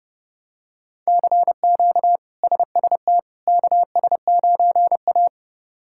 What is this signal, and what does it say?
Morse code, heard as a keyed tone, about 30 wpm, 700 Hz. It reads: CQ SST KS9A
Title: K1USN SST QSO Practice - 30wpm